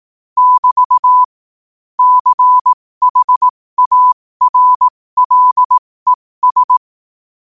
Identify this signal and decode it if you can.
Morse code, heard as a keyed tone, about 18 wpm, spaced as for 15 wpm, 1000 Hz. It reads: <BT> CHARLES
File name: qotdfiles / 15wpm